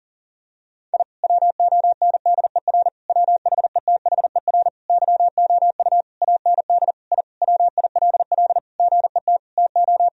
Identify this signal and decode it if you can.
Morse code, heard as a keyed tone, about 40 wpm, 700 Hz. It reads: I WONDER WHETHER YOU AND I WILL GET TO
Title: Sentences from Top 1000 Words - Review Part 3 - 40wpm